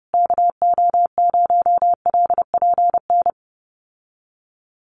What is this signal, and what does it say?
Morse code, heard as a keyed tone, about 30 wpm, 700 Hz. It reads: KO0LPN
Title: USA Call Signs Repeated 3x 30wpm